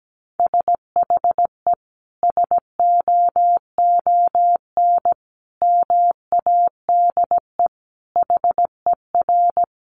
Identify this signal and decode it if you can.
Morse code, heard as a keyed tone, about 17 wpm, 700 Hz. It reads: SHE SOON MADE HER
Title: Sentences from Top 1100 Words 17wpm